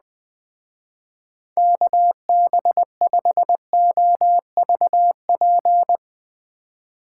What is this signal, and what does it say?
Morse code, heard as a keyed tone, about 20 wpm, 700 Hz. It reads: KB5OVP
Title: Speed Racing - USA 2x3 Call Signs 17wpm